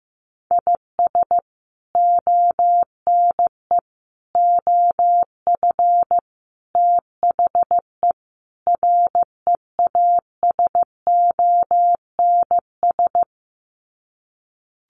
Morse code, heard as a keyed tone, about 15 wpm, 700 Hz. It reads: IS ONE OF THE REASONS
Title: Sets of 5 Words Encore2 15wpm